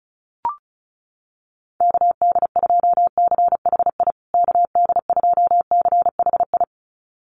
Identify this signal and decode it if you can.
Morse code, heard as a keyed tone, about 34 wpm, 700 Hz. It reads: KD2CHI KD2CHI
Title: USA Call Signs Repeated 2x 35wpm